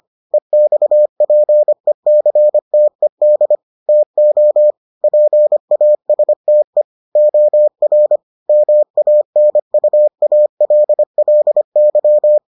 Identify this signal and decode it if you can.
Morse code, heard as a keyed tone, about 25 wpm, 600 Hz. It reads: EXPECTED TO PASTE OR MANUALLY